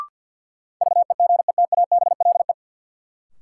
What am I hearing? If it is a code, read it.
Morse code, heard as a keyed tone, about 52 wpm, 700 Hz. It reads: VEGETABLE
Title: Sentences from Top 1300 Words - New Word Drill Alt - 50wpm